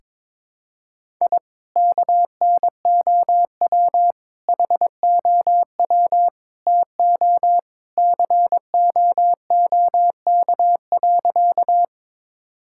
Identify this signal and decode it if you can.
Morse code, heard as a keyed tone, about 22 wpm, 700 Hz. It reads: I KNOW HOW TO COOK.
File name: practice / 22wpm / Sentences with 3-Word Phrases - 22wpm